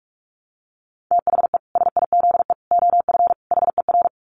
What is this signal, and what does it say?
Morse code, heard as a keyed tone, about 44 wpm, 700 Hz. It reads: THE SIZE OF HER
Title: Dit Challenge 45wpm